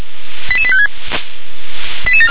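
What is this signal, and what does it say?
Five-tone selective call (ZVEI), heard as quick runs of five tones, tones of about 70 ms. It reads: 88856
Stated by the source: ZVEI five-tone call 88690 received by SDR.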